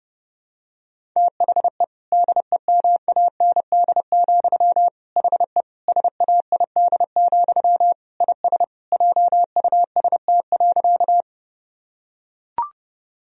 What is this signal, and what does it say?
Morse code, heard as a keyed tone, about 32 wpm, 700 Hz. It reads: THE DEMAND, HE SAID, IS JUST.
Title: Sentences from Top 1000 Words - Review Part 4 - 30wpm